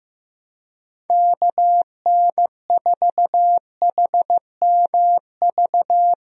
Morse code, heard as a keyed tone, about 15 wpm, 700 Hz. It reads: KN4HMV
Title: USA Call Signs - 6 Characters - Repeated 3x - 15wpm